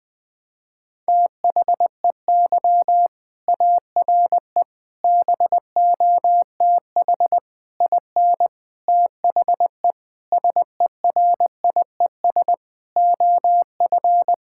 Morse code, heard as a keyed tone, about 20 wpm, 700 Hz. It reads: THEY ARE BOTH IN THE SERIES OF
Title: Sentences from Top 1000 Words - Review Part 5 - 20wpm